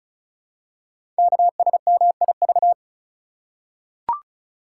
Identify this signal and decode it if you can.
Morse code, heard as a keyed tone, about 36 wpm, 690 Hz. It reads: KSMIV
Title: Code Groups - 5 Letters - 35wpm